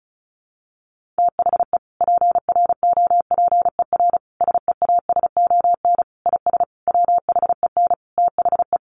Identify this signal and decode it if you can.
Morse code, heard as a keyed tone, about 34 wpm, 700 Hz. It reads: THE PROPER SEASON IS WHEN THE
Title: Sentences from Top 1400 Words 35wpm